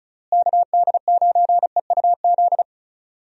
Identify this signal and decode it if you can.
Morse code, heard as a keyed tone, about 36 wpm, 700 Hz. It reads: KD9EUZ
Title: USA Call Signs - 6 Characters - Repeated 3x - 35wpm